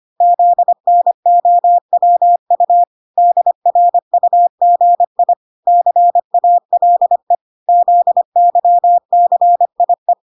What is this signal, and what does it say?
Morse code, heard as a keyed tone, about 25 wpm, 700 Hz. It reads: ZNOWU DRUGI CALE ZYCIE